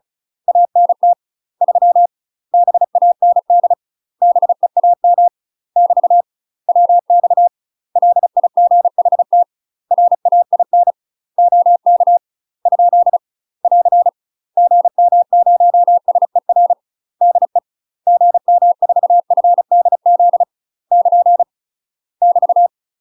Morse code, heard as a keyed tone, about 36 wpm, 700 Hz. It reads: ANT 3 BAND BEAM <BT> WX LIGHT RAIN OK ? <AR> GM0SER DE GM4FDZ <KN> <BT>
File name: UK MASTER QSOs TRAINING / 35WPM / UK MASTER QSOs 0026